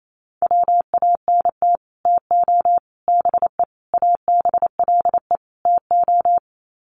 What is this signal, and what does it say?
Morse code, heard as a keyed tone, about 28 wpm, 700 Hz. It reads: WANT TO BE ABLE TO
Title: Sets of 5 Words 28wpm